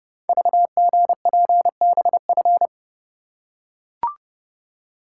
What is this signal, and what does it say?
Morse code, heard as a keyed tone, about 30 wpm, 700 Hz. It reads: VGPBF
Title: Code Groups - 5 Letters - 30wpm